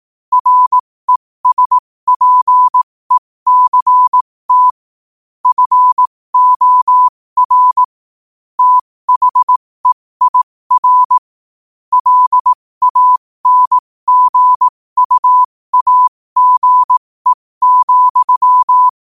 Morse code, heard as a keyed tone, about 18 wpm, spaced as for 15 wpm, 1000 Hz. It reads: RESPECT FOR THEIR LANGUAGE,
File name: qotdfiles / 15wpm